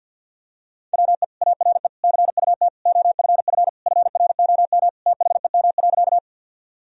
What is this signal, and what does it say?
Morse code, heard as a keyed tone, about 51 wpm, 700 Hz. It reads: WE ARE CUT OFF FROM THEM.
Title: Sentences from Top 1000 Words - Review Part 1 - 50wpm